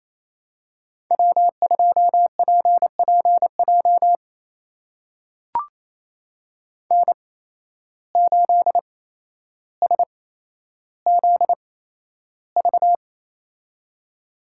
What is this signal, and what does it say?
Morse code, heard as a keyed tone, about 28 wpm, spaced as for 12 wpm, 700 Hz. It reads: W2PPJ N8SZV
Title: USA Call Signs - 5 Characters - 8x Character Spacing - 28wpm